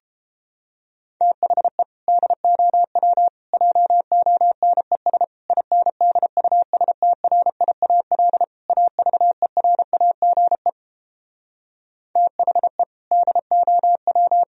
Morse code, heard as a keyed tone, about 33 wpm, 700 Hz. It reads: THE DOW JONES INDUSTRIAL AVERAGE THE DOW